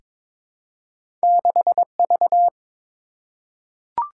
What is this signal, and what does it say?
Morse code, heard as a keyed tone, about 22 wpm, 700 Hz. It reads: 6V
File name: practice / 22wpm / Two Letter-Number Combinations - 22wpm